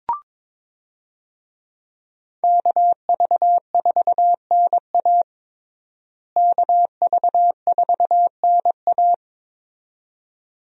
Morse code, heard as a keyed tone, about 22 wpm, 700 Hz. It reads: KV4NA KV4NA
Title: USA Call Signs - 5 Characters - Repeated 2x - 22wpm